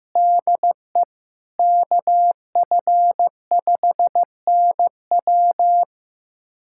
Morse code, heard as a keyed tone, about 15 wpm, 700 Hz. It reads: DE KF5NW
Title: USA Callsign DE Callsign - 5 Characters - 15wpm